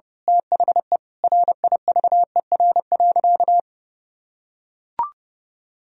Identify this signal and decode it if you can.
Morse code, heard as a keyed tone, about 30 wpm, 700 Hz. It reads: THE RIVER.
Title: Sentences from Top 1000 Words - Review Part 1 - 30wpm